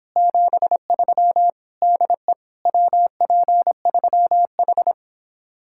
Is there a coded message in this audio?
Yes: Morse code from 0.2 to 4.9 s, heard as a keyed tone, about 26 wpm, 700 Hz. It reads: 73 DE WP3H